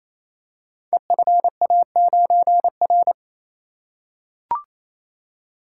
Morse code, heard as a keyed tone, about 27 wpm, 700 Hz. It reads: EFA9R
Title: Code Groups - 5 Letter:Numbers - 28wpm